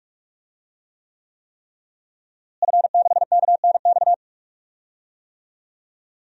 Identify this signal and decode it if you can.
Morse code, heard as a keyed tone, about 46 wpm, 700 Hz. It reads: F6KNX
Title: DXCC-100 Call Signs - 5 Characters - Repeated 3x - 45wpm